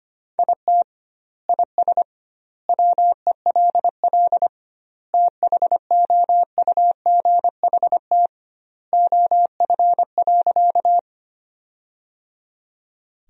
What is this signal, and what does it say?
Morse code, heard as a keyed tone, about 25 wpm, 700 Hz. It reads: IT IS WELL THOUGHT OF.
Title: 2x Word-Spacing - Sentences from Top 300 Words - 25wpm